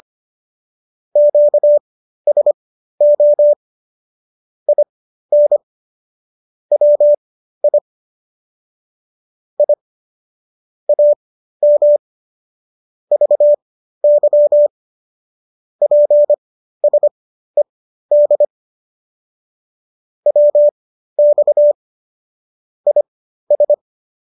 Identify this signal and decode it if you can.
Morse code, heard as a keyed tone, about 25 wpm, spaced as for 13 wpm, 590 Hz. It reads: QSO IN WI I AM VY PSED WX IS